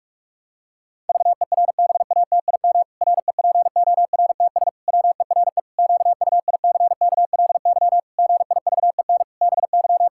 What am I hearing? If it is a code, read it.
Morse code, heard as a keyed tone, about 41 wpm, 700 Hz. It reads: VERBATIM REPORTS WERE QUICKLY GIVEN BY
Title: Pangram Challenge v2 - 45wpm